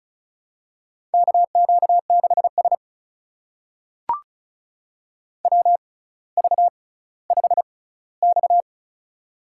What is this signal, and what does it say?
Morse code, heard as a keyed tone, about 36 wpm, spaced as for 18 wpm, 700 Hz. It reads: KQ6S WV5X
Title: USA Call Signs - 4 Characters - 6x Character Spacing - 35wpm